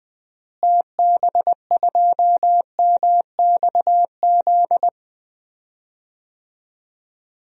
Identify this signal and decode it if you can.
Morse code, heard as a keyed tone, about 20 wpm, 700 Hz. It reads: TB2MXZ
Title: DXCC-100 Call Signs - 6 Characters - Repeated 2x - 20wpm